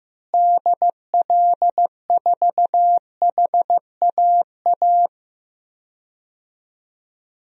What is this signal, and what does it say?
Morse code, heard as a keyed tone, about 15 wpm, 700 Hz. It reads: DL4HAA
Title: DXCC-100 Call Signs - 6 Characters - Repeated 3x - 15wpm